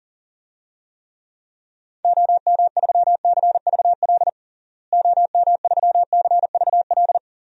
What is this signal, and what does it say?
Morse code, heard as a keyed tone, about 40 wpm, 700 Hz. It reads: OM3CVL OM3CVL
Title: DXCC-100 Call Signs - 6 Characters - Repeated 3x - 40wpm